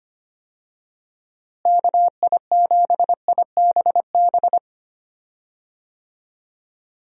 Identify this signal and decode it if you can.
Morse code, heard as a keyed tone, about 25 wpm, 700 Hz. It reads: KI7IBB
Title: USA Call Signs - 6 Characters - Repeated 2x - 25wpm